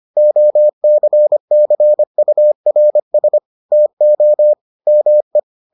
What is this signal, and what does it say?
Morse code, heard as a keyed tone, about 25 wpm, 600 Hz. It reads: OCCURS TO ME